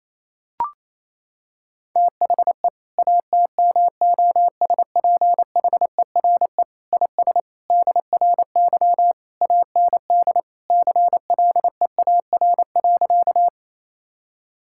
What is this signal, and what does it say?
Morse code, heard as a keyed tone, about 28 wpm, 700 Hz. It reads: THE ATMOSPHERE IS DRY AND CLEAR.
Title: Sentences from Top 1600 Words 28wpm